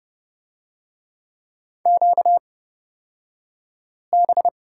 Morse code, heard as a keyed tone, about 30 wpm, spaced as for 5 wpm, 700 Hz. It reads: QB
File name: Lesson 38d - Letter q - Two Characters 30wpm